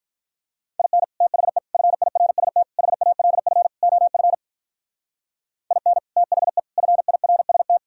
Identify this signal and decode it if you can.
Morse code, heard as a keyed tone, about 43 wpm, 700 Hz. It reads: IN THE FIRST HALF OF IN THE FIRST
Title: Speed-Racing - Sets of 5 Words 35wpm